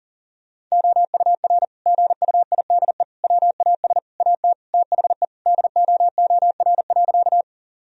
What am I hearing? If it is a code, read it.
Morse code, heard as a keyed tone, about 37 wpm, 700 Hz. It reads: OUR GUIDE WAS AT THE DOOR.
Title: Sentences from Top 1000 Words - Review Part 3 - 40wpm